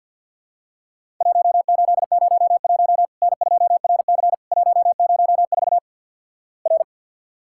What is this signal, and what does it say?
Morse code, heard as a keyed tone, about 50 wpm, 700 Hz. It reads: 1801 N2RC 104 R
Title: Worked All Europe DX Contest - QTC Exchange - 50wpm